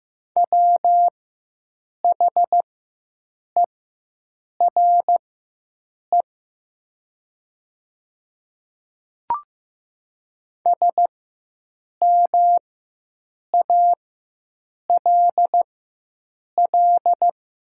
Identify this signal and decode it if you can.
Morse code, heard as a keyed tone, about 15 wpm, spaced as for 7 wpm, 700 Hz. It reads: WHERE SMALL
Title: 4x Character Spacing - Top 300 Words - 15wpm